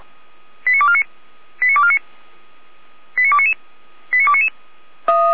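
Five-tone selective call (ZVEI), heard as quick runs of five tones, tones of about 70 ms. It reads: 89249 89249 89299 89299
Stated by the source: ZVEI five-tone selective call received on SDR.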